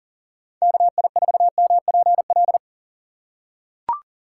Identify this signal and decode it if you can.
Morse code, heard as a keyed tone, about 37 wpm, 700 Hz. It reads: KI4MPL
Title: USA Call Signs - 6 Characters - 4x Character Spacing - 40wpm